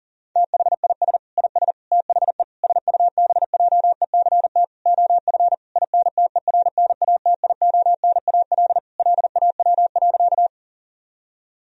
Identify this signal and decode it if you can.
Morse code, heard as a keyed tone, about 40 wpm, 700 Hz. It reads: THIS IS THE SUBJECT OF INTERNATIONAL LAW.
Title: Sentences from Top 1000 Words - Review Part 4 - 40wpm